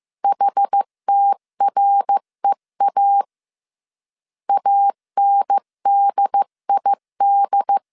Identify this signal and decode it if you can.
Morse code, heard as a keyed tone, about 15 wpm, 800 Hz. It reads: HTREA ANDID
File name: CW / DET / 1 3 1500C S A 14wpm 2447